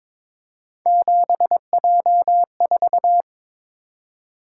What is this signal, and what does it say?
Morse code, heard as a keyed tone, about 26 wpm, 700 Hz. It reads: 7J4
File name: Speed-Racing - Three Letter-Number Combinations - 22wpm